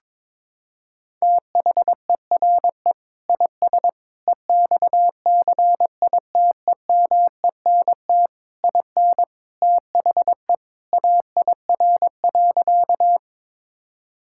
Morse code, heard as a keyed tone, about 22 wpm, 700 Hz. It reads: THERE IS EXCITEMENT IN THE AIR.
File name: practice / 22wpm / Sentences with 3-Word Phrases - 22wpm